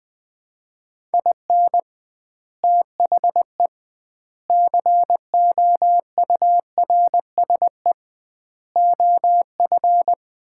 Morse code, heard as a keyed tone, about 21 wpm, 700 Hz. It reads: IN THE COURSE OF
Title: 2x Word-Spacing - Sets of 4 Words - 20wpm